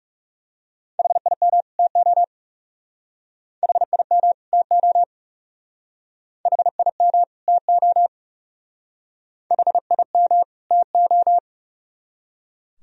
Morse code, heard as a keyed tone, about 40 wpm, 700 Hz. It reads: HIM TO HIM TO HIM TO HIM TO